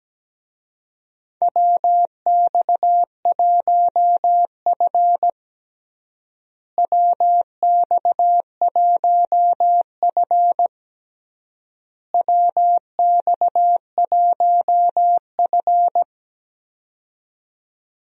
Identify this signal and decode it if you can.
Morse code, heard as a keyed tone, about 17 wpm, 700 Hz. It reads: WX1F WX1F WX1F
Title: USA Call Signs - 4 Characters - Repeated 3x - 17wpm